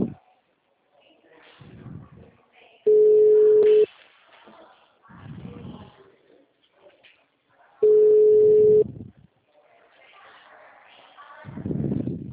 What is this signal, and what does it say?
Ringback tone (Europe, 425 Hz).